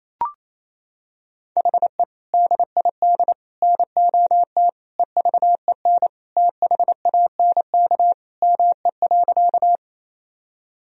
Morse code, heard as a keyed tone, about 28 wpm, 700 Hz. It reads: HE DID NOT EVEN THANK ME.
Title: Sentences from Top 1000 Words - Review Part 1 - 28wpm